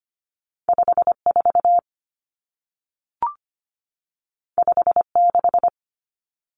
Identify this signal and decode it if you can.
Morse code, heard as a keyed tone, about 25 wpm, 700 Hz. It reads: 54 56
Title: Common Cut Numbers - 2-Digit 25wpm